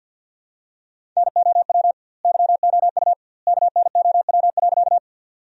Morse code, heard as a keyed tone, about 50 wpm, 700 Hz. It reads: NOW YOU KNOW.
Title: 2x Word-Spacing - Sentences from Top 100 Words - 50wpm